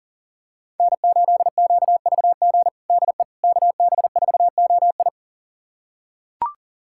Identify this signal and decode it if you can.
Morse code, heard as a keyed tone, about 40 wpm, 700 Hz. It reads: N8QVG DE KB4OI
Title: USA Callsign DE Callsign - 5 Characters - 40wpm